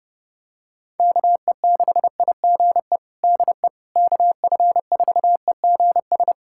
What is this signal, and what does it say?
Morse code, heard as a keyed tone, about 29 wpm, 700 Hz. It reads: KE6IGE DE KF4EGS
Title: USA Callsign DE Callsign - 6 Characters - 30wpm